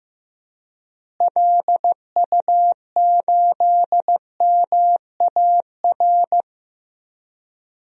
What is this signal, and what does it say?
Morse code, heard as a keyed tone, about 15 wpm, 700 Hz. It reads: LU8MAR
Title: DXCC-100 Call Signs - 6 Characters - Repeated 2x - 15wpm